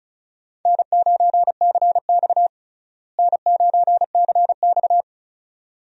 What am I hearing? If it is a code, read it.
Morse code, heard as a keyed tone, about 34 wpm, 700 Hz. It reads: N9CX N9CX
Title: USA Call Signs - 4 Characters - Repeated 2x - 35wpm